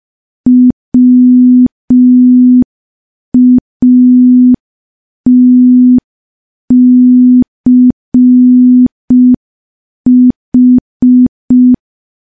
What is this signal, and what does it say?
Morse code, heard as a keyed tone, about 5 wpm, 250 Hz. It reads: WATCH
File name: morse / morse-training-05WPM